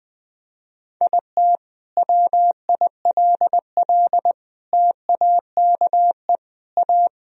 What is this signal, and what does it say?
Morse code, heard as a keyed tone, about 20 wpm, 700 Hz. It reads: IT WILL TAKE A
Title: Sentences from Top 1000 Words - Review Part 4 - 20wpm